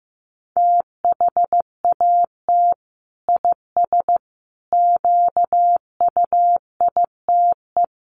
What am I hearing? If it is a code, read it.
Morse code, heard as a keyed tone, about 15 wpm, 700 Hz. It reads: THAT IS QUITE
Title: Sentences from Top 1100 Words 15wpm